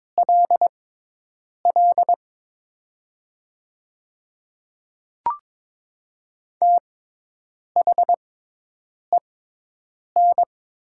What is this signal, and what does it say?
Morse code, heard as a keyed tone, about 22 wpm, spaced as for 8 wpm, 700 Hz. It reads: LL THEN
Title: 6x Character Spacing - Top 200 Words - 22wpm